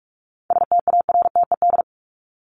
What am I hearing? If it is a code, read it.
Morse code, heard as a keyed tone, about 44 wpm, 700 Hz. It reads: STARTED
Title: ED Suffix - Top 100 Words with Base Words 45wpm